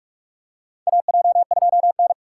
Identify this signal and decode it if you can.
Morse code, heard as a keyed tone, about 44 wpm, 700 Hz. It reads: AJ2N
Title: USA Call Signs - 4 Characters - 2x Character Spacing - 45wpm